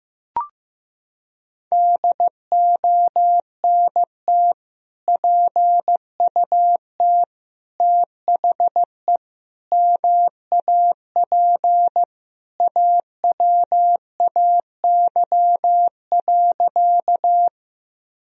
Morse code, heard as a keyed tone, about 15 wpm, 700 Hz. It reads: DONT PUT THE MAP AWAY.
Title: Sentences from Top 1000 Words - Review Part 2 - 15wpm